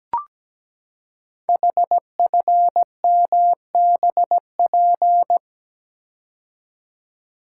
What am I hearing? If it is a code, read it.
Morse code, heard as a keyed tone, about 17 wpm, 700 Hz. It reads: HFMBP
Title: Code Groups - 5 Letter:Numbers - 17wpm